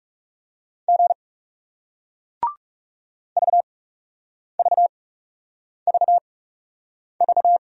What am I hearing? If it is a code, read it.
Morse code, heard as a keyed tone, about 40 wpm, spaced as for 8 wpm, 700 Hz. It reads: G VVVV